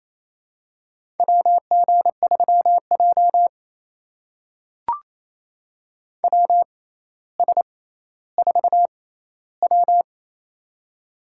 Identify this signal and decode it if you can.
Morse code, heard as a keyed tone, about 28 wpm, spaced as for 14 wpm, 700 Hz. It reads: WG3J WS4W
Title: USA Call Signs - 4 Characters - 6x Character Spacing - 28wpm